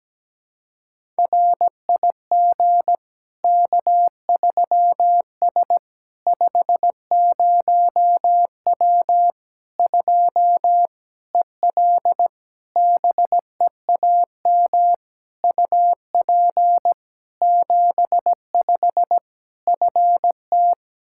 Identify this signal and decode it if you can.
Morse code, heard as a keyed tone, about 17 wpm, 700 Hz. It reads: RIG K3S 50W 2 EL BEAM UP 75 FT
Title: Practice QSO - Exchange 2 - Info - 17wpm